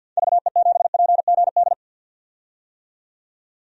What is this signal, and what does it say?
Morse code, heard as a keyed tone, about 49 wpm, 700 Hz. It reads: VE7PGD
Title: DXCC-100 Call Signs - 6 Characters - 50wpm